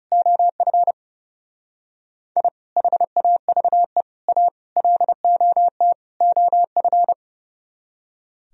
Morse code, heard as a keyed tone, about 36 wpm, 700 Hz. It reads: OF I HAVE A LOT OF
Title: Speed-Racing - Sets of 5 Words 30wpm